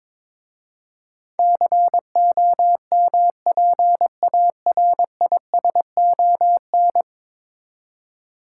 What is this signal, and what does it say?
Morse code, heard as a keyed tone, about 22 wpm, 700 Hz. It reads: COMPARISON
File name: practice / 22wpm / 10 Letter Words - First 3 Letters Given - 22wpm